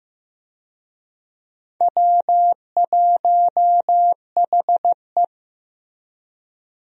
Morse code, heard as a keyed tone, about 15 wpm, 700 Hz. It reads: W1HE
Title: USA Call Signs - 4 Characters - Repeated 3x - 15wpm